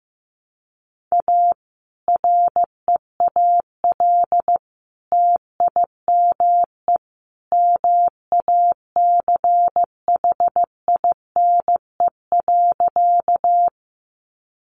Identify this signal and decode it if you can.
Morse code, heard as a keyed tone, about 15 wpm, 700 Hz. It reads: A REAL TIME MACHINE.
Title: A Sound Of Thunder 15wpm